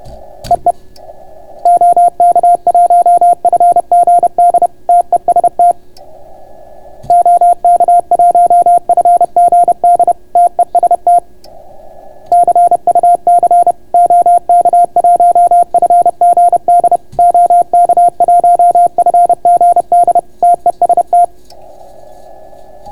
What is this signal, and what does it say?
Morse code, heard as a keyed tone, about 31 wpm, 690 Hz. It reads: EE OK1FGD TEST OK1FGD TEST CUC OK1FGD OK1FGD TEST